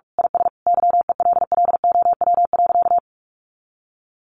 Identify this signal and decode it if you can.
Morse code, heard as a keyed tone, about 44 wpm, 700 Hz. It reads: IS YELLOW.
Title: Additional Easy Phrases 45wpm